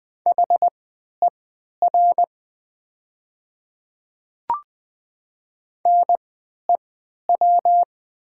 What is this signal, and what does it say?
Morse code, heard as a keyed tone, about 20 wpm, spaced as for 11 wpm, 700 Hz. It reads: HER NEW
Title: 3x Character Spacing - Top 200 Words - 20wpm